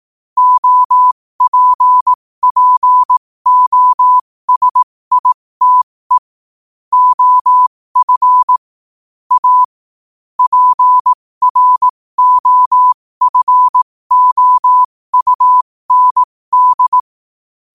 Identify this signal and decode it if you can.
Morse code, heard as a keyed tone, about 18 wpm, spaced as for 15 wpm, 1000 Hz. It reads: OPPOSITE OF A PROFOUND
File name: qotdfiles / 15wpm